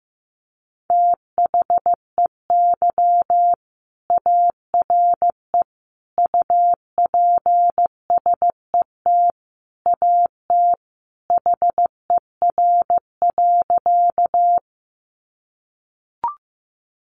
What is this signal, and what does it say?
Morse code, heard as a keyed tone, about 15 wpm, 700 Hz. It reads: THEY ARE UPSET AT HER.
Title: Sentences from Top 800 Words 15wpm